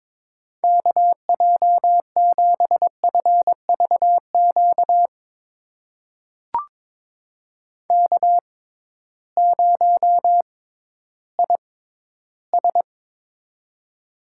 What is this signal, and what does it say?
Morse code, heard as a keyed tone, about 22 wpm, spaced as for 13 wpm, 700 Hz. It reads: KJ7FVQ K0IS
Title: USA Call Signs - 6x Character Spacing - 22wpm